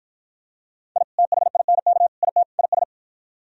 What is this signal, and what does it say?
Morse code, heard as a keyed tone, about 53 wpm, 700 Hz. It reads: I THINK IT IS